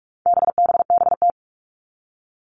Morse code, heard as a keyed tone, about 44 wpm, 700 Hz. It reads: BBBT
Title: All Cut Numbers - 4-Digit 45wpm